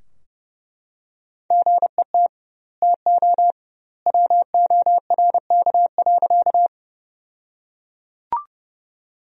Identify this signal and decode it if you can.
Morse code, heard as a keyed tone, about 31 wpm, 700 Hz. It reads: GET TO WORK.
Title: 2x Word-Spacing - Sentences from Top 200 Words - 30wpm